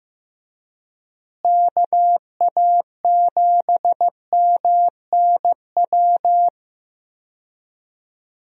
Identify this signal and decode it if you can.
Morse code, heard as a keyed tone, about 15 wpm, 700 Hz. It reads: KA7MNW
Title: USA Call Signs - 6 Characters - Repeated 3x - 15wpm